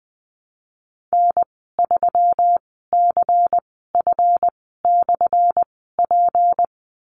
Morse code, heard as a keyed tone, about 20 wpm, spaced as for 14 wpm, 700 Hz. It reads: N3CF/P
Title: 179 Lesson 29g - Slash - Call Signs 20wpm